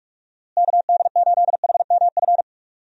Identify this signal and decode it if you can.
Morse code, heard as a keyed tone, about 41 wpm, 700 Hz. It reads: KD8HMF
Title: USA Call Signs - 6 Characters - Repeated 2x - 45wpm